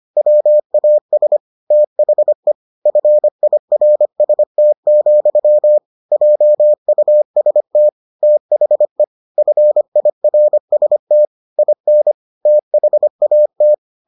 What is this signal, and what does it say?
Morse code, heard as a keyed tone, about 25 wpm, 600 Hz. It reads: WAS THE FIRST, JUST THE FIRST IN THAT